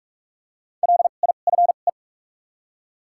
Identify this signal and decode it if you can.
Morse code, heard as a keyed tone, about 44 wpm, 700 Hz. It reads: L I F E
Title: 2x Character Spacing - Top 300 Words - 45wpm